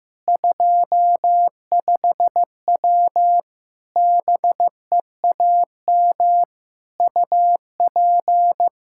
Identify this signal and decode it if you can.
Morse code, heard as a keyed tone, about 16 wpm, 700 Hz. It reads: 25W BEAM UP
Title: Practice QSO - Exchange 2 - Info - 15wpm